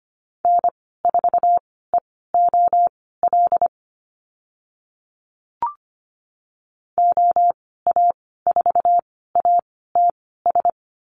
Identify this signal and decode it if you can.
Morse code, heard as a keyed tone, about 25 wpm, spaced as for 16 wpm, 700 Hz. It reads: N4EOL OA4ATS
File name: Lesson 15g - Number 5 - Call Signs 25wpm